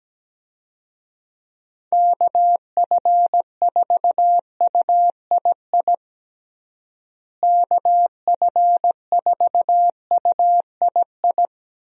Morse code, heard as a keyed tone, about 17 wpm, 700 Hz. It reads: KF4UII KF4UII
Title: USA Call Signs - 6 Characters - Repeated 2x - 17wpm